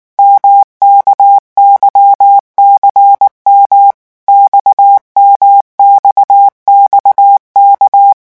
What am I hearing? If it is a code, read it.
Morse code, heard as a keyed tone, about 19 wpm, 800 Hz. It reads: MKYCM XMXXK
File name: revision1-vitesse 18 mots